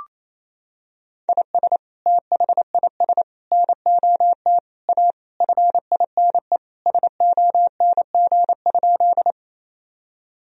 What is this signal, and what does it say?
Morse code, heard as a keyed tone, about 27 wpm, 700 Hz. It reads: IS THIS NOT A FINE SONG?
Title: Sentences from Top 1000 Words - Review Part 2 - 28wpm